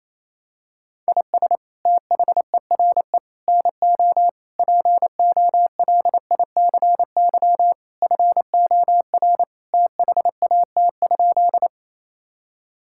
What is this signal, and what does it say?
Morse code, heard as a keyed tone, about 28 wpm, 700 Hz. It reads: IS THERE NO POLICY FOR THAT?
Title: Sentences from Top 1000 Words - Review Part 5 - 28wpm